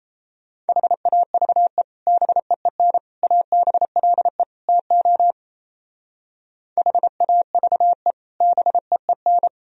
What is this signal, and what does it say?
Morse code, heard as a keyed tone, about 35 wpm, 700 Hz. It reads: HAVE BEEN ABLE TO HAVE BEEN
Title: Speed-Racing - Sets of 4 Words 28wpm